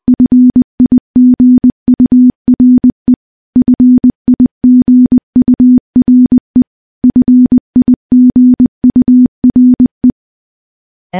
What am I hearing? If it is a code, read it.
Morse code, heard as a keyed tone, about 20 wpm, 250 Hz. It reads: FIGURE FIGURE FIGURE
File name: morse-training-20WPM